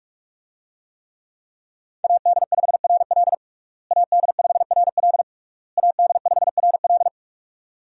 Morse code, heard as a keyed tone, about 41 wpm, 700 Hz. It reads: AD5RL AD5RL AD5RL
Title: USA Call Signs - 5 Characters - Repeated 3x - 45wpm